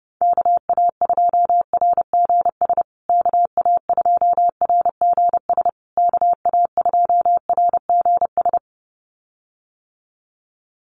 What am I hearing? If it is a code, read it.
Morse code, heard as a keyed tone, about 30 wpm, 700 Hz. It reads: KA2RGS KA2RGS KA2RGS
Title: USA Call Signs Repeated 3x 30wpm